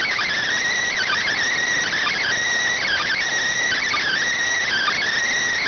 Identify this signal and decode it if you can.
Emergency Alert System (SAME) tones (header bursts), heard as harsh buzzing bursts. It is heard from 0.8 to 5.0 s.